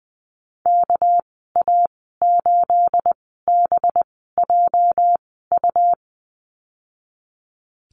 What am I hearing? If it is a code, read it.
Morse code, heard as a keyed tone, about 20 wpm, spaced as for 14 wpm, 700 Hz. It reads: KA8BJU
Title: 213 Lesson 34g - Number 8 - Call Signs 20wpm